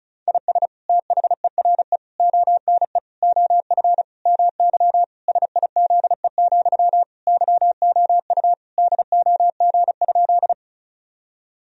Morse code, heard as a keyed tone, about 36 wpm, 700 Hz. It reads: IS THERE ONE OF MY SIZE, YOU DOG?
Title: Sentences from Top 1000 Words - Review Part 2 - 35wpm